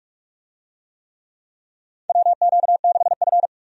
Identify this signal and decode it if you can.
Morse code, heard as a keyed tone, about 45 wpm, 700 Hz. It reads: WQ6F
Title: USA Call Signs - 4 Characters - 45wpm